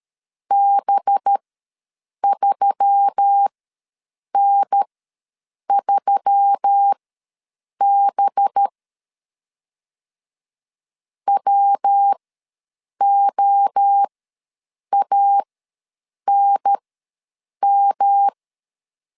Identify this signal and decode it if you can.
Morse code, heard as a keyed tone, about 13 wpm, spaced as for 7 wpm, 800 Hz. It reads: B3N3B WOANM
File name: 5 LB ws 12wpm